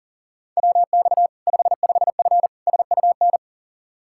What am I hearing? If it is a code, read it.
Morse code, heard as a keyed tone, about 40 wpm, 700 Hz. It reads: WX 55F SUN
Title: K1USN SST QSO Practice 40wpm